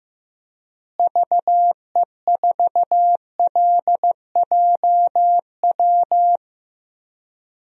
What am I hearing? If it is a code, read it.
Morse code, heard as a keyed tone, about 15 wpm, 700 Hz. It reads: VE4LJW
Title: DXCC-100 Call Signs - 6 Characters - Repeated 2x - 15wpm